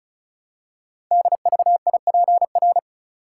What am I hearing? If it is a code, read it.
Morse code, heard as a keyed tone, about 36 wpm, 690 Hz. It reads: DVIPR
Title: Code Groups - 5 Letters - 35wpm